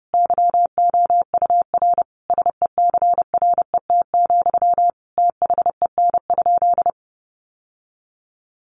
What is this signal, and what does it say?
Morse code, heard as a keyed tone, about 30 wpm, 700 Hz. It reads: YOUR SECRET, THEN?
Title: Sentences from Top 1200 Words 30wpm